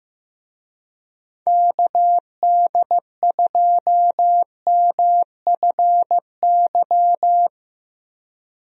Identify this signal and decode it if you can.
Morse code, heard as a keyed tone, about 15 wpm, 700 Hz. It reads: KD2MFY
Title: USA Call Signs - 6 Characters - 15wpm